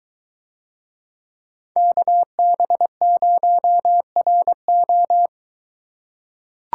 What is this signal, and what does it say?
Morse code, heard as a keyed tone, about 21 wpm, 700 Hz. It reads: KB0RO